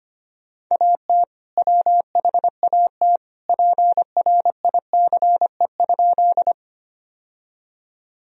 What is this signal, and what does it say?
Morse code, heard as a keyed tone, about 25 wpm, 700 Hz. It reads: AT WHAT PRICE?
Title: Sentences from Top 1000 Words - Review Part 4 - 25wpm